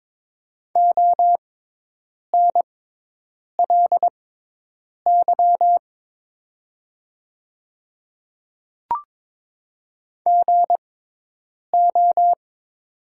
Morse code, heard as a keyed tone, about 21 wpm, spaced as for 8 wpm, 700 Hz. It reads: ONLY GO
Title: 6x Character Spacing - Top 200 Words - 22wpm